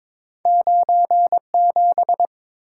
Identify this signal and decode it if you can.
Morse code, heard as a keyed tone, about 23 wpm, 700 Hz. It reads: 97
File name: practice / 22wpm / Two Letter-Number Combinations - 22wpm